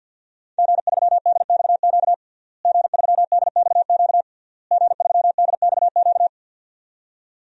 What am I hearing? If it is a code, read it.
Morse code, heard as a keyed tone, about 50 wpm, 700 Hz. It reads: G3DXQ G3DXQ G3DXQ
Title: DXCC-100 Call Signs - 5 Characters - Repeated 3x - 50wpm